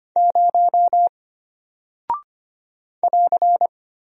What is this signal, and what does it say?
Morse code, heard as a keyed tone, about 25 wpm, 700 Hz. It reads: 0 <AR>
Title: Single Character (Letter:Number:Punctuation:Prosign) Alt 25wpm